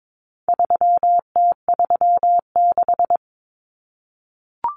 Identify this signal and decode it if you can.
Morse code, heard as a keyed tone, about 22 wpm, 700 Hz. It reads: 3T36
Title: Common Cut Numbers - 4-Digit 22wpm